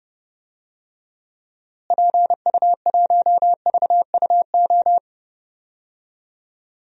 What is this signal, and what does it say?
Morse code, heard as a keyed tone, about 30 wpm, 700 Hz. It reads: PU1VUO
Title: DXCC-100 Call Signs - 6 Characters - Repeated 3x - 30wpm